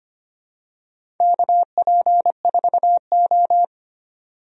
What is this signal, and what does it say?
Morse code, heard as a keyed tone, about 25 wpm, 700 Hz. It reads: KP4O
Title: DXCC-100 Call Signs - 4 Characters - 25wpm